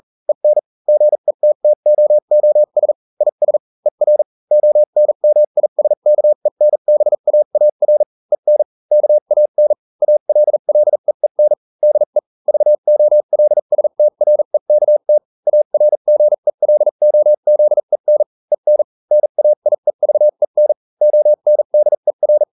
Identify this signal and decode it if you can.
Morse code, heard as a keyed tone, about 40 wpm, spaced as for 35 wpm, 600 Hz. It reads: EN GETTOOS IS ER ONMISKENBAAR EN KAN ALLEEN DE VOLSTREKT ARGELOZEN EN NAIEVEN ONDER